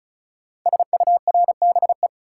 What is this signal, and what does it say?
Morse code, heard as a keyed tone, about 36 wpm, 700 Hz. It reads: SURBE
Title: Code Groups - 5 Letters - 35wpm